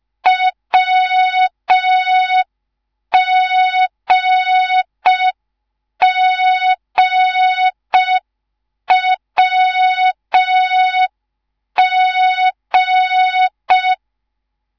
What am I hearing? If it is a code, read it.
Morse code, heard as a keyed tone, about 5 wpm, 3700 Hz. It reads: WGGWG
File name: Morze-HA2MN / hang / 10GWZ HA2MN